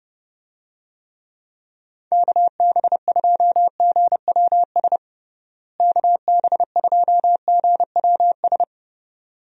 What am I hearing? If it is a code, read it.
Morse code, heard as a keyed tone, about 32 wpm, 700 Hz. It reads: KB2GWS KB2GWS
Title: USA Call Signs - 6 Characters - Repeated 2x - 30wpm